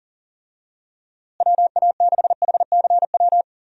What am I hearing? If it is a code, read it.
Morse code, heard as a keyed tone, about 40 wpm, 700 Hz. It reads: WA6HCW
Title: USA Call Signs - 6 Characters - Repeated 2x - 40wpm